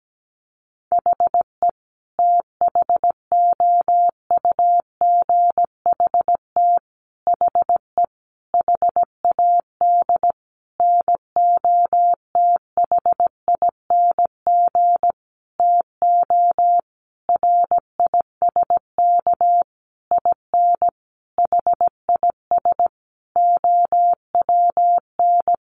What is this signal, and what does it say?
Morse code, heard as a keyed tone, about 17 wpm, 700 Hz. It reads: HE THOUGHT HE HAD NOTHING TO RISK IN HIS OWN
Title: Sentences from Top 600 Words 17wpm